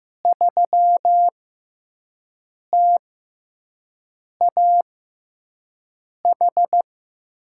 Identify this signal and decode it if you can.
Morse code, heard as a keyed tone, about 15 wpm, spaced as for 5 wpm, 700 Hz. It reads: 3TAH
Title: USA Call Signs - 5 Characters - 6x Character Spacing - 15wpm